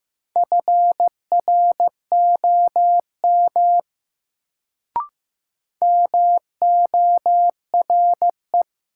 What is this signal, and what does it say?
Morse code, heard as a keyed tone, about 15 wpm, 700 Hz. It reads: FROM MORE
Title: Sentences from Top 100 Words - New Word Drill - 15wpm